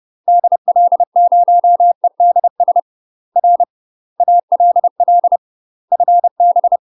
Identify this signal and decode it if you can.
Morse code, heard as a keyed tone, about 30 wpm, 700 Hz. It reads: DL0EDS R ALL FB
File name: UK MASTER QSOs TRAINING/30WPM